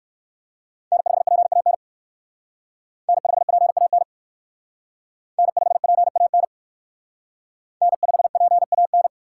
Morse code, heard as a keyed tone, about 56 wpm, 700 Hz. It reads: N5PAN N5PAN N5PAN N5PAN